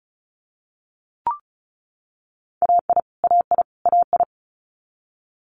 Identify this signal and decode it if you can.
Morse code, heard as a keyed tone, about 36 wpm, 700 Hz. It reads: AI AI AI
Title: Multi-letter Phonemes Repeated 3x 35wpm